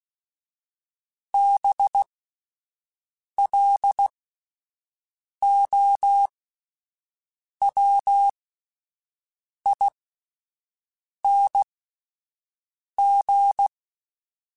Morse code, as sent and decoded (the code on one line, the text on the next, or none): -... .-.. --- .-- .. -. --.
BLOWING